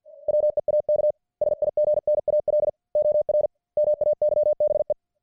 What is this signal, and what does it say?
Morse code, heard as a keyed tone, about 50 wpm, 600 Hz. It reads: WEAK SIGNAL OR MAYBE